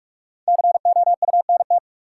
Morse code, heard as a keyed tone, about 47 wpm, 700 Hz. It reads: COUNT
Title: Sentences from Top 1500 Words - New Word Drill - 45wpm